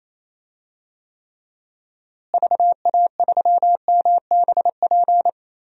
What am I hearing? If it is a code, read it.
Morse code, heard as a keyed tone, about 28 wpm, 700 Hz. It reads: VA3MBP
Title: DXCC-100 Call Signs - 6 Characters - Repeated 2x - 28wpm